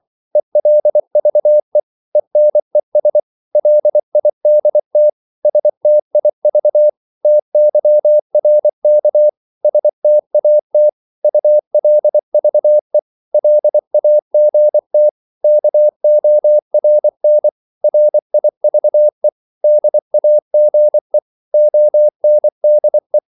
Morse code, heard as a keyed tone, about 24 wpm, 600 Hz. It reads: ELVE ENES LIDT STIV TYRK STAT ULVE LAGT KORN RIVE DAGE ONDE
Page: Korte danske ord 24wpm